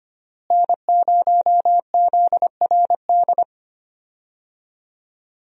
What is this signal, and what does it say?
Morse code, heard as a keyed tone, about 25 wpm, 700 Hz. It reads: N0ZRD
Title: USA Call Signs - 5 Characters - Repeated 2x - 25wpm